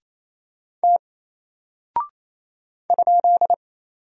Morse code, heard as a keyed tone, about 28 wpm, 700 Hz. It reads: T ?
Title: Single Character (Letter:Number:Punctuation:Prosign) Alt 28wpm